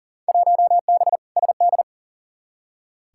Morse code, heard as a keyed tone, about 40 wpm, 700 Hz. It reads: 1B SD
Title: Field Day Exchanges Encore - 40wpm